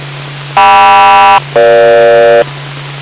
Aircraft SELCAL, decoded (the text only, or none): KM-EG